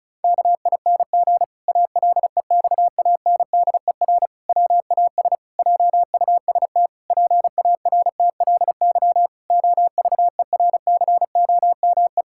Morse code, heard as a keyed tone, about 36 wpm, 700 Hz. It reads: KING ALEXANDER WAS JUST PARTLY OVERCOME
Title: Pangram Challenge v2 - 35wpm